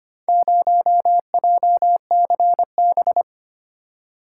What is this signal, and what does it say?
Morse code, heard as a keyed tone, about 25 wpm, 700 Hz. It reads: 0JCB
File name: practice / 25wpm / Four Letter-Number Combinations - 25wpm